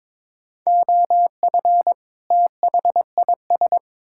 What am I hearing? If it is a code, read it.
Morse code, heard as a keyed tone, about 22 wpm, 700 Hz. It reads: OF THIS
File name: practice / 22wpm / Build A Phrase - 5-Word Phrases - 22wpm